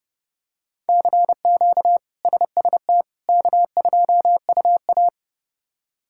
Morse code, heard as a keyed tone, about 30 wpm, 700 Hz. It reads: CQ SST K2UA
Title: K1USN SST QSO Practice - 30wpm